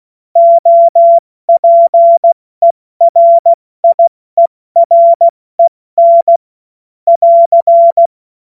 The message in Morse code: OPERIEREN <AR>

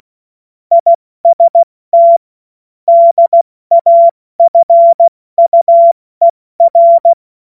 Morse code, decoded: IST DAFUER

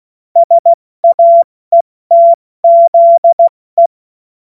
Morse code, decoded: SAETZE

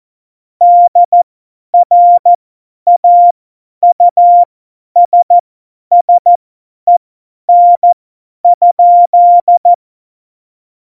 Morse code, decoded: DRAUSSEN?

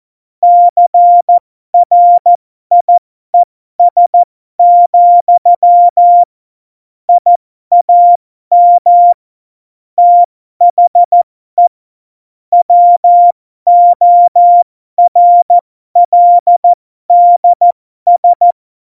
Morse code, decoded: CRIES, I AM THE WORLDS

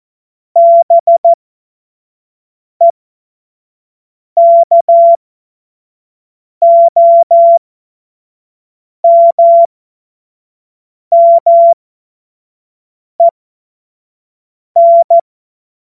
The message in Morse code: BEKOMMEN

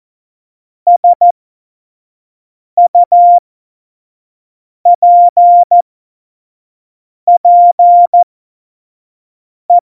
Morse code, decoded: SUPPE

